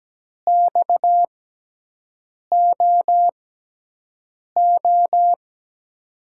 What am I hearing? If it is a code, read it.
Morse code, heard as a keyed tone, about 17 wpm, spaced as for 6 wpm, 700 Hz. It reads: XOO
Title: USA Call Signs - 5 Characters - 6x Character Spacing - 17wpm